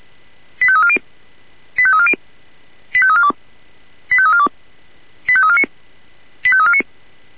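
Five-tone selective call (ZVEI), heard as quick runs of five tones, tones of about 70 ms. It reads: 85340 85340 85342 85342 85349 85349
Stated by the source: ZVEI five-tone selective call received on an SDR.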